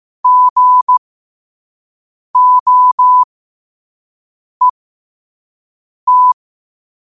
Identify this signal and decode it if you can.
Morse code, heard as a keyed tone, about 15 wpm, spaced as for 5 wpm, 1000 Hz. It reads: GOET